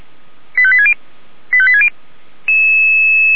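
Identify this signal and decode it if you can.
Five-tone selective call (ZVEI), heard as quick runs of five tones, tones of about 70 ms. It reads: 85860 85860
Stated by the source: ZVEI five-tone selective call received by SDR.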